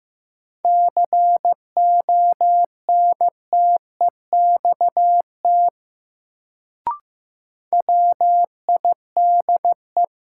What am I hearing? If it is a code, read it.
Morse code, heard as a keyed tone, about 15 wpm, 700 Hz. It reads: CONTEXT WIDE
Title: Sets of 4 Words - New Word Drill - 15wpm